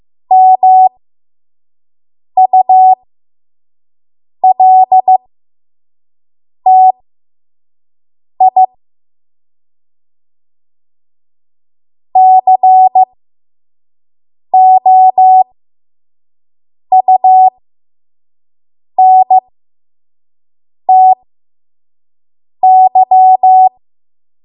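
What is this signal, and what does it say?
Morse code, heard as a keyed tone, about 15 wpm, spaced as for 5 wpm, 750 Hz. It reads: MULTI COUNTY
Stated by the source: 5 WPM Morse code practice.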